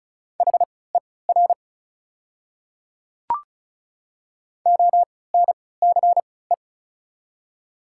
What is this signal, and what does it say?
Morse code, heard as a keyed tone, about 36 wpm, spaced as for 20 wpm, 690 Hz. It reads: HER ONCE
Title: 3x Character Spacing - Top 300 Words - 35wpm